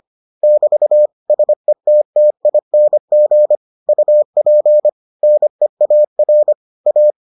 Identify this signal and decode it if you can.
Morse code, heard as a keyed tone, about 25 wpm, 600 Hz. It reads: <BT> SETTING UP NEAR A